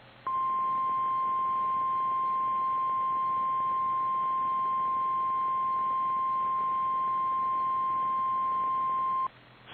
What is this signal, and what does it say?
Emergency Alert System (SAME) tones (the 1050 Hz attention tone). It is heard from 0.2 to 9.3 s.
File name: eas / KKPRKXPN / KKPRKXPN